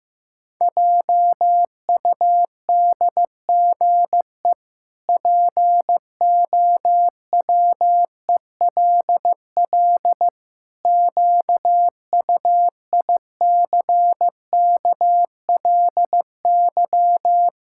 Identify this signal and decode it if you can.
Morse code, heard as a keyed tone, about 16 wpm, 700 Hz. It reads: JUDGE POWELL QUICKLY
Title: Pangram Challenge v2 - 15wpm